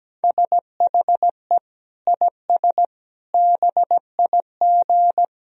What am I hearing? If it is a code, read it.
Morse code, heard as a keyed tone, about 17 wpm, 700 Hz. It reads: SHE IS BIG
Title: Sentences from Top 1000 Words - Review Part 1 - 17wpm